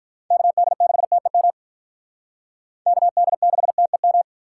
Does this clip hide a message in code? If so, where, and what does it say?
Yes: Morse code from 0.3 to 4.2 s, heard as a keyed tone, about 46 wpm, 700 Hz. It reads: KD6TEM KD6TEM